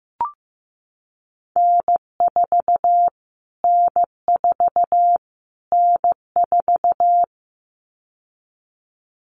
Morse code, heard as a keyed tone, about 15 wpm, 700 Hz. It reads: N4 N4 N4
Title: Callsign Prefixes for Top 25 DXCC - Repeated 3x - 15wpm